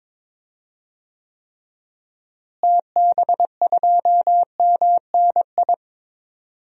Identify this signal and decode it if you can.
Morse code, heard as a keyed tone, about 22 wpm, 700 Hz. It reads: TB2MNI
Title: DXCC-100 Call Signs - 6 Characters - Repeated 2x - 22wpm